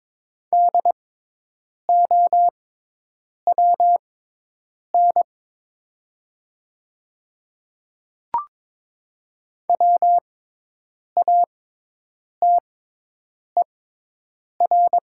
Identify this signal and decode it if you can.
Morse code, heard as a keyed tone, about 22 wpm, spaced as for 8 wpm, 700 Hz. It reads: DOWN WATER
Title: 6x Character Spacing - Top 100 Words - 22wpm